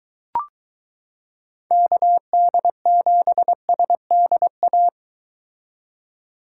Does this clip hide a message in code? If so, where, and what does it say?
Yes: Morse code from 1.7 to 4.9 s, heard as a keyed tone, about 23 wpm, 700 Hz. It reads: KD7SDA